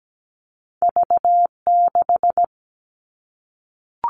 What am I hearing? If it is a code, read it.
Morse code, heard as a keyed tone, about 17 wpm, 700 Hz. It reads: V6
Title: Two Challenging Letter-Number Combinations 17wpm